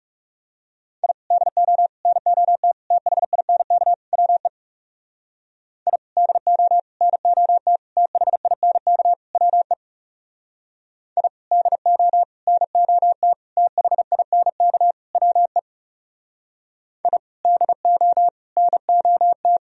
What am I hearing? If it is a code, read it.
Morse code, heard as a keyed tone, about 38 wpm, 700 Hz. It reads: I DO NOT THINK WE I DO NOT THINK WE I DO NOT THINK WE I DO NOT